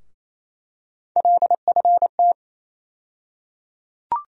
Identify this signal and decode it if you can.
Morse code, heard as a keyed tone, about 28 wpm, 700 Hz. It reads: LFT
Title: USA Call Sign Postfixes - 3 Letters 28wpm